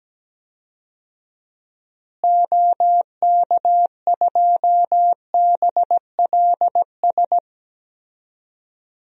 Morse code, heard as a keyed tone, about 17 wpm, 700 Hz. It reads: OK2BLS
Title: DXCC-100 Call Signs - 6 Characters - Repeated 2x - 17wpm